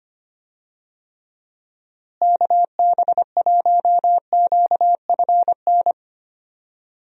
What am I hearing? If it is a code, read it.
Morse code, heard as a keyed tone, about 25 wpm, 700 Hz. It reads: KB1QFN
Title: USA Call Signs - 6 Characters - Repeated 2x - 25wpm